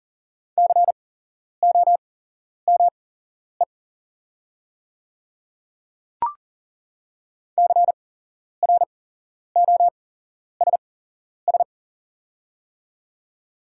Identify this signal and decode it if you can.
Morse code, heard as a keyed tone, about 40 wpm, spaced as for 11 wpm, 700 Hz. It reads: COME CROSS
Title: 8x Character Spacing - Top 500 Words - 40wpm